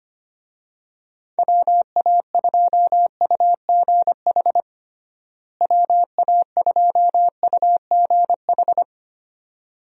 Morse code, heard as a keyed tone, about 25 wpm, 700 Hz. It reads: WA2UGH WA2UGH
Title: USA Call Signs - 6 Characters - Repeated 2x - 25wpm